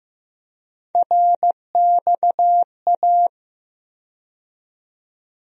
Morse code, heard as a keyed tone, about 15 wpm, 700 Hz. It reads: RXA
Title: USA Call Sign Postfixes - 3 Letters 15wpm